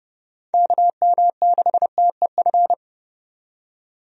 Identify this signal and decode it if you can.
Morse code, heard as a keyed tone, about 30 wpm, 700 Hz. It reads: KM6TEF
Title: USA Call Signs - 6 Characters - 30wpm